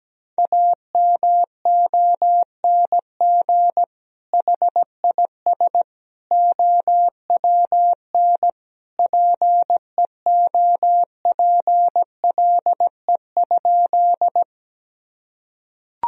Morse code, heard as a keyed tone, about 17 wpm, 700 Hz. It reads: AMONG HIS OWN PEOPLE?
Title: Sentences from Top 1000 Words - Review Part 3 - 17wpm